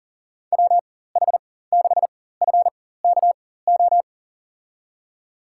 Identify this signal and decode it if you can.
Morse code, heard as a keyed tone, about 40 wpm, spaced as for 19 wpm, 700 Hz. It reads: WH6FKO
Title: USA Call Signs - 6 Characters - 4x Character Spacing - 40wpm